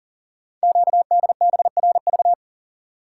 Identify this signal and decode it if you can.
Morse code, heard as a keyed tone, about 40 wpm, 700 Hz. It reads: QDBRV
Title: Code Groups - 5 Letters - 40wpm